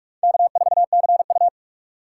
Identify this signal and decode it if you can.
Morse code, heard as a keyed tone, about 46 wpm, 700 Hz. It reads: K4CU
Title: USA Call Signs - 4 Characters - 2x Character Spacing - 45wpm